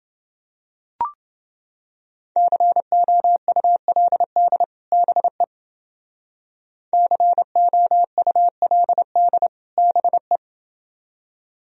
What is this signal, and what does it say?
Morse code, heard as a keyed tone, about 29 wpm, 700 Hz. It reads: COULD BE COULD BE